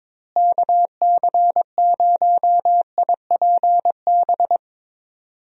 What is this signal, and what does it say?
Morse code, heard as a keyed tone, about 21 wpm, 700 Hz. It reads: KC0IPB
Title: USA Call Signs - 6 Characters - Repeated 2x - 22wpm